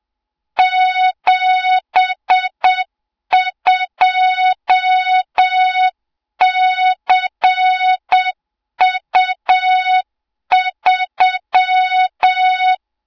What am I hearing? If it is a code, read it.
Morse code, heard as a keyed tone, about 8 wpm, 3700 Hz, with noise. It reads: 72CU3
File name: Morze-HA2MN / hang / 20 GYAKORLAT